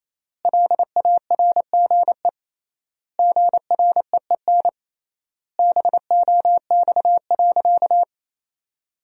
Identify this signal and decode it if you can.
Morse code, heard as a keyed tone, about 28 wpm, 700 Hz. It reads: LARGE GREEN BOX.
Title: 3x Word-Spacing - Sentences from Top 500 Words - 28wpm